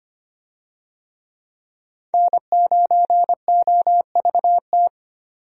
Morse code, heard as a keyed tone, about 25 wpm, 700 Hz. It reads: N9OVT
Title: USA Call Signs - 5 Characters - 25wpm